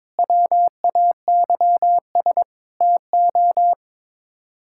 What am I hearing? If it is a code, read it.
Morse code, heard as a keyed tone, about 21 wpm, 700 Hz. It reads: WAYS TO
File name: practice / 22wpm / Sets of 6 Words - 22wpm